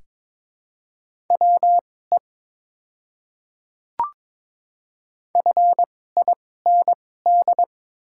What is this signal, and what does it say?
Morse code, heard as a keyed tone, about 23 wpm, spaced as for 16 wpm, 700 Hz. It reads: WE FIND
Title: 2x Character Spacing - Top 200 Words - 22wpm